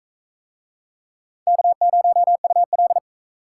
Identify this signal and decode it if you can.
Morse code, heard as a keyed tone, about 40 wpm, 700 Hz. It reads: K0UL